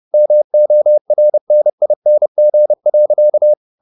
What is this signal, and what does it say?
Morse code, heard as a keyed tone, about 30 wpm, 600 Hz. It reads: MORNING.